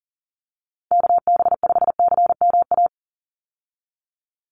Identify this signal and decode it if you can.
Morse code, heard as a keyed tone, about 37 wpm, 700 Hz. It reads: KB5CMA
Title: USA Call Signs 40wpm